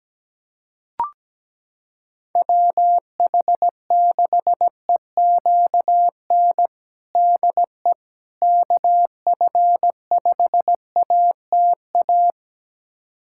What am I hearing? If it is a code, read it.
Morse code, heard as a keyed tone, about 17 wpm, 700 Hz. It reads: WH6EQN DE KF5ATA
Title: USA Callsign DE Callsign - 6 Characters - 17wpm